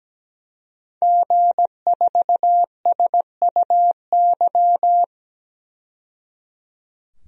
Morse code, heard as a keyed tone, about 17 wpm, 700 Hz. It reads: G4SUY
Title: DXCC-100 Call Signs - 5 Characters - 17wpm